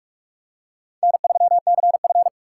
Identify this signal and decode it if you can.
Morse code, heard as a keyed tone, about 46 wpm, 700 Hz. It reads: N3CF
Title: USA Call Signs - 4 Characters - 6x Character Spacing - 45wpm